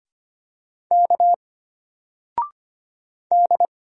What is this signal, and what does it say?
Morse code, heard as a keyed tone, about 25 wpm, 700 Hz. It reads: K D
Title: Single Letter-Number 25wpm